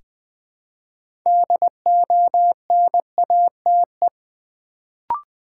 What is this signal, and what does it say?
Morse code, heard as a keyed tone, about 20 wpm, 700 Hz. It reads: DONATE
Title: Build A Word - 2 Letters At A Time - 6-Letter Words - 20wpm